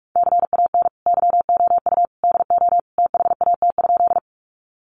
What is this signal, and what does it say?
Morse code, heard as a keyed tone, about 44 wpm, 700 Hz. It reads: CAN YOU DO THAT?
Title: Sentences from Top 100 Words 45wpm